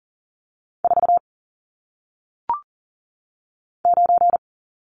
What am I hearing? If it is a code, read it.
Morse code, heard as a keyed tone, about 39 wpm, 700 Hz. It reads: 4 9
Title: Numbers - 1-Digit 40wpm